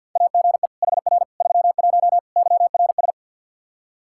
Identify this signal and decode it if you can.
Morse code, heard as a keyed tone, about 52 wpm, 700 Hz. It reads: AGE HR 31 YRS
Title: Practice QSO - Exchange 3 - More Info - 50wpm